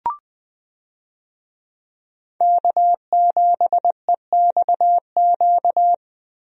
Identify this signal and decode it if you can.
Morse code, heard as a keyed tone, about 21 wpm, 700 Hz. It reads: K7EXQ
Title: USA Call Signs - 5 Characters - Repeated 3x - 20wpm